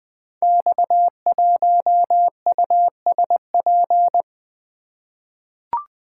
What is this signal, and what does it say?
Morse code, heard as a keyed tone, about 20 wpm, 700 Hz. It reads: X1USP
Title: Code Groups - 5 Letter:Numbers - 20wpm